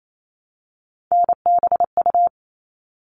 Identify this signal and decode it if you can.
Morse code, heard as a keyed tone, about 28 wpm, 700 Hz. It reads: NBU
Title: All Cut Numbers - 3-Digit 28wpm